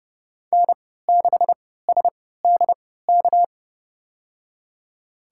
Morse code, heard as a keyed tone, about 30 wpm, spaced as for 17 wpm, 700 Hz. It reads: N6SDK
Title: USA Call Signs - 5 Characters - 3x Character Spacing - 30wpm